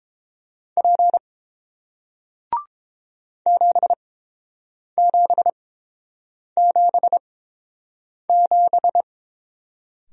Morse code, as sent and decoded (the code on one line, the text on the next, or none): .--. / --... --... --... --...
P 7777